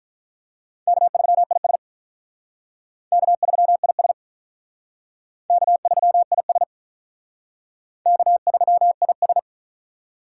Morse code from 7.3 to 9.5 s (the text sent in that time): K3IS